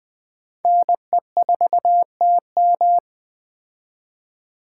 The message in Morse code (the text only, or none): NE4TM